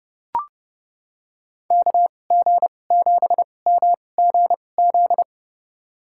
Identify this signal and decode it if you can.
Morse code, heard as a keyed tone, about 31 wpm, spaced as for 22 wpm, 700 Hz. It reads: KG7MGZ
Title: USA Call Signs - 6 Characters - 2x Character Spacing - 30wpm